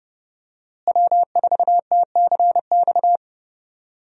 Morse code, heard as a keyed tone, about 31 wpm, 700 Hz. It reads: W4TCX
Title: USA Call Signs - 5 Characters - Repeated 2x - 30wpm